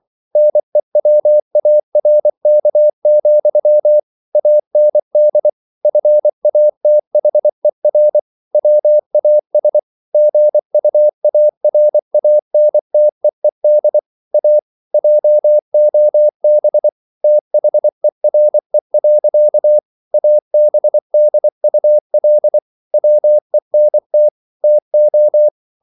Morse code, heard as a keyed tone, about 24 wpm, 600 Hz. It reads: NEWARK, AND FATHER WAS GUARANTEED A JOB THERE. ABDUL WENT TO